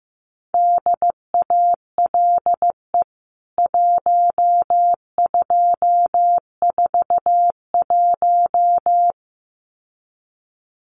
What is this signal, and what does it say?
Morse code, heard as a keyed tone, about 15 wpm, 700 Hz. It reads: DALE 1241
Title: CWT Exchange 15wpm